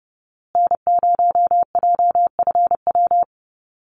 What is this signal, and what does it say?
Morse code, heard as a keyed tone, about 29 wpm, 700 Hz. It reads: N0JFW
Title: USA Call Signs Repeated 2x 30wpm